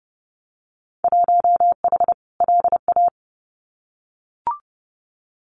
Morse code, heard as a keyed tone, about 30 wpm, 700 Hz. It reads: 1H LA
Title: Winter Field Day Exchanges 30wpm